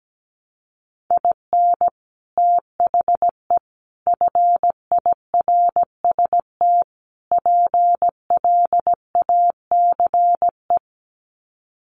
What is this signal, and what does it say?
Morse code, heard as a keyed tone, about 17 wpm, 700 Hz. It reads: IN THE FIRST PLACE
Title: Sets of 4 Words 17wpm